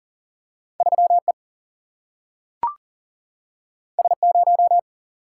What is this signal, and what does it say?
Morse code, heard as a keyed tone, about 40 wpm, 700 Hz. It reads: 3 E S 0
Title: Single-Letter Single-Number 40wpm